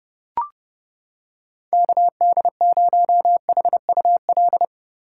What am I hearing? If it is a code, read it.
Morse code, heard as a keyed tone, about 30 wpm, 700 Hz. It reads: KD0HUL